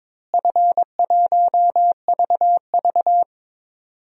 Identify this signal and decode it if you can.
Morse code, heard as a keyed tone, about 21 wpm, 700 Hz. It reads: F1VV
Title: DXCC-100 Call Signs - 4 Characters - Repeated 3x - 22wpm